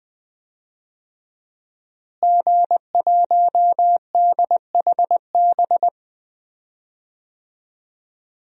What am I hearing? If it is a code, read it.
Morse code, heard as a keyed tone, about 20 wpm, 700 Hz. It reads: G1DHB
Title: DXCC-100 Call Signs - 5 Characters - Repeated 3x - 20wpm